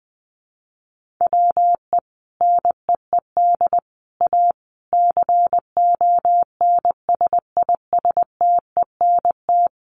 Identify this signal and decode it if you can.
Morse code, heard as a keyed tone, about 20 wpm, 700 Hz. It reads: WE NEED A CONSISTENT
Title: Sentences from Top 1500 Words 20wpm